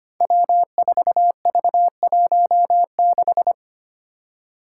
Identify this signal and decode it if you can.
Morse code, heard as a keyed tone, about 25 wpm, 700 Hz. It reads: W4V16
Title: Five Challenging Letter-Number Combinations 25wpm